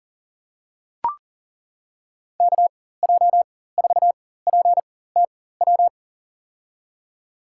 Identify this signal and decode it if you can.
Morse code, heard as a keyed tone, about 40 wpm, spaced as for 19 wpm, 700 Hz. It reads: KJ4PTW
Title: USA Call Signs - 6 Characters - 4x Character Spacing - 40wpm